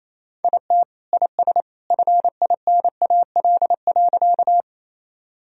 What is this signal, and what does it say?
Morse code, heard as a keyed tone, about 28 wpm, 700 Hz. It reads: IT IS FINAL.
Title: Sentences from Top 1000 Words - Review Part 2 - 28wpm